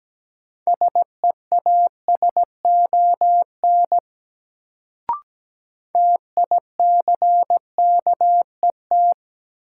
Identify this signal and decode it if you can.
Morse code, heard as a keyed tone, about 17 wpm, 700 Hz. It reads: SEASON TICKET
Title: Sentences from Top 1400 Words - New Word Drill - 17wpm